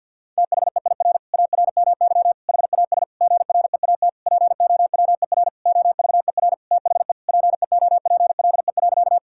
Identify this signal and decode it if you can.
Morse code, heard as a keyed tone, about 52 wpm, 690 Hz. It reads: THEIR ARMY HAS GREAT POWER OVER THE PEOPLE.
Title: Sentences from Top 1000 Words - Review Part 5 - 50wpm